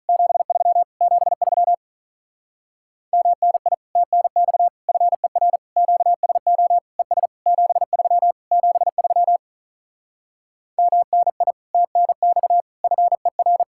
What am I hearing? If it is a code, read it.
Morse code, heard as a keyed tone, about 48 wpm, 700 Hz. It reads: 73 73 MNI TNX FER QSO ES 73 73 MNI TNX FER